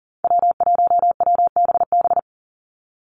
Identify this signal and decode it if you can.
Morse code, heard as a keyed tone, about 40 wpm, 700 Hz. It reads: W1WBB
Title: Call Signs Repeated 2x - 40wpm